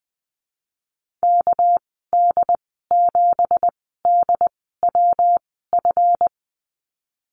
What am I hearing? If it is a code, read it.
Morse code, heard as a keyed tone, about 20 wpm, spaced as for 14 wpm, 700 Hz. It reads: KD7DWF
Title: 231 Lesson 37g - Letter x - Call Signs 20wpm